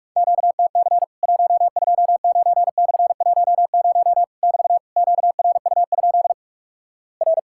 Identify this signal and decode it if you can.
Morse code, heard as a keyed tone, about 46 wpm, 700 Hz. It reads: QTC 129/10 <BT> QRU? R
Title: Worked All Europe DX Contest - QTC Exchange - 45wpm